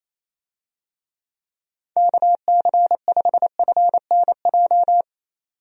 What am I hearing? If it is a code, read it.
Morse code, heard as a keyed tone, about 28 wpm, 700 Hz. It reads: KC5FNJ
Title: USA Call Signs - 6 Characters - 28wpm